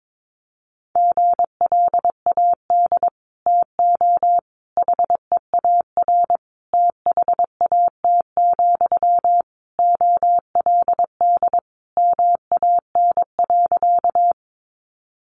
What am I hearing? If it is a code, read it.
Morse code, heard as a keyed tone, about 22 wpm, 700 Hz. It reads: GLAD TO HEAR THAT, OLD MAN.
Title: Sentences from Top 800 Words 22wpm